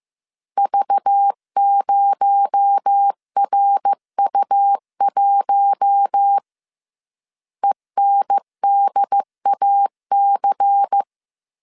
Morse code, heard as a keyed tone, about 15 wpm, 800 Hz. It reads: V0RU1 ENDAC